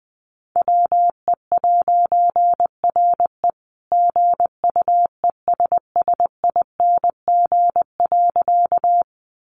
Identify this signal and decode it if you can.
Morse code, heard as a keyed tone, about 20 wpm, 700 Hz. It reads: WE'RE GUESSING.
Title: A Sound Of Thunder 20wpm